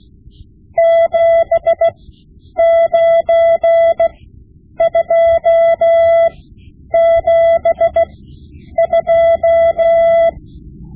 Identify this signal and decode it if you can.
Morse code, heard as a keyed tone, about 16 wpm, spaced as for 9 wpm, 670 Hz. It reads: 79272
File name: Morse Lessons / Morse Lessons / Tape-5B